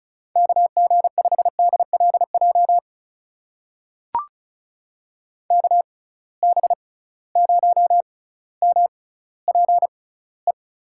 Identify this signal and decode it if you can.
Morse code, heard as a keyed tone, about 36 wpm, spaced as for 18 wpm, 700 Hz. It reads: KG5DLJ KB0MPE
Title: USA Call Signs - 6 Characters - 6x Character Spacing - 35wpm